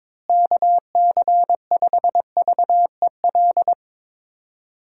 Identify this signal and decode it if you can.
Morse code, heard as a keyed tone, about 22 wpm, 700 Hz. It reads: KC5VEL
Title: USA Call Signs - 6 Characters - Repeated 3x - 22wpm